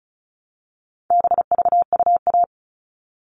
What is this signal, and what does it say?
Morse code, heard as a keyed tone, about 34 wpm, 700 Hz. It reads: BVUA
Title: All Cut Numbers - 4-Digit 35wpm